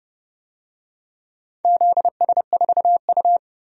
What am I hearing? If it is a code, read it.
Morse code, heard as a keyed tone, about 29 wpm, 700 Hz. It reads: ZS4U
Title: DXCC-100 Call Signs - 4 Characters - 30wpm